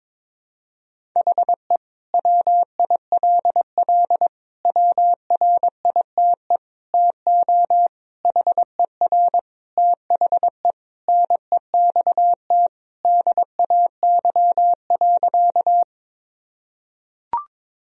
Morse code, heard as a keyed tone, about 22 wpm, 700 Hz. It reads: HE WILL WRITE TO HER THE NEXT DAY.
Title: Sentences from Top 1000 Words - Review Part 1 - 22wpm